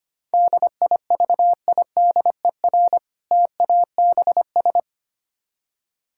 Morse code, heard as a keyed tone, about 25 wpm, 700 Hz. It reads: DIVIDER TABS
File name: practice / 25wpm / Difficult Words - Speed-Racing with Hints - 25wpm